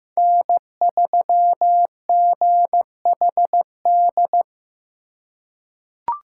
Morse code, heard as a keyed tone, about 15 wpm, 700 Hz. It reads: N3GHD
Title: USA Call Signs - 5 Characters - 4x Character Spacing - 15wpm